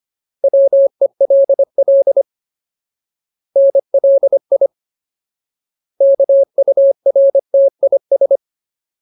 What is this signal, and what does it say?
Morse code, heard as a keyed tone, about 25 wpm, 550 Hz. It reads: WELL NLI KURTIS
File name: Ultimate Word List / 25WPM Ultimate